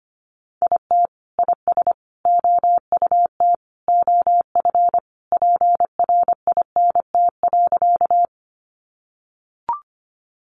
Morse code, heard as a keyed tone, about 25 wpm, 700 Hz. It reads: IT IS OUT OF PRINT.
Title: Sentences from Top 1400 Words 25wpm